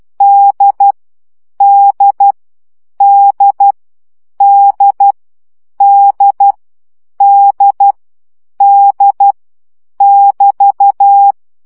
Morse code, heard as a keyed tone, about 12 wpm, 800 Hz. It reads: D D D D D D D <BT>
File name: CW gyakorlas d 10 baud